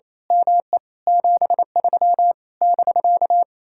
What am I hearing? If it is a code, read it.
Morse code, heard as a keyed tone, about 26 wpm, 700 Hz. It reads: ME 73 <BK>
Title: Speed-Racing Common QSO Phrases - 28wpm